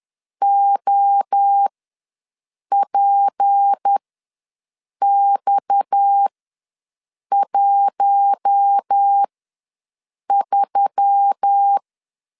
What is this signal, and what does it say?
Morse code, heard as a keyed tone, about 10 wpm, spaced as for 6 wpm, 800 Hz. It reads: OPX13
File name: LEZIONI / 6 PKX0 / 6 PKX ws 10wpm 3706